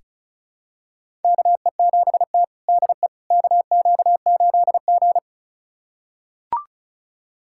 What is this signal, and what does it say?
Morse code, heard as a keyed tone, about 36 wpm, 700 Hz. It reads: KE7T DE KQ8G
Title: USA Callsign DE Callsign - 4 Characters - 35wpm